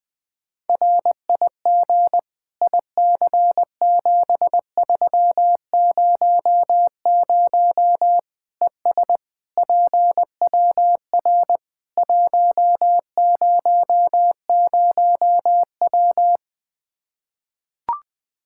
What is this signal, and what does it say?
Morse code, heard as a keyed tone, about 20 wpm, 700 Hz. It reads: RIG IC7300 ES PWR 100W
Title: Practice QSO - Exchange 2 - Info - 20wpm